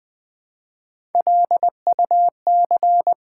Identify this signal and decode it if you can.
Morse code, heard as a keyed tone, about 20 wpm, 700 Hz. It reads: LUC
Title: Build A Word - 1 Letter At A Time - 4-Letter Words - 20wpm